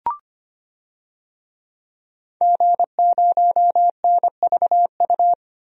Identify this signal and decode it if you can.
Morse code, heard as a keyed tone, about 25 wpm, 700 Hz. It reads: G0NVU
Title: DXCC-100 Call Signs - 5 Characters - 25wpm